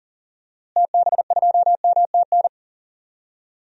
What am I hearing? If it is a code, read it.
Morse code, heard as a keyed tone, about 40 wpm, 700 Hz. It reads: TB2MTN
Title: DXCC-100 Call Signs - 6 Characters - Repeated 2x - 40wpm